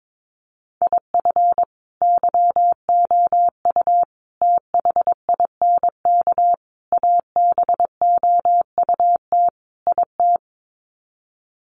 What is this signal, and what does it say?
Morse code, heard as a keyed tone, about 22 wpm, 700 Hz. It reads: IF YOU THINK ABOUT IT
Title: Sets of 5 Words Encore2 22wpm